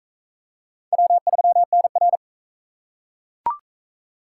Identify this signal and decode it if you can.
Morse code, heard as a keyed tone, about 40 wpm, 700 Hz. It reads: W3NR